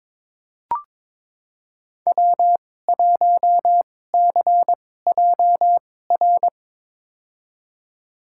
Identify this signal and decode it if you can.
Morse code, heard as a keyed tone, about 22 wpm, spaced as for 16 wpm, 700 Hz. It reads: W1CJR
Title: USA Call Signs - 5 Characters - 2x Character Spacing - 22wpm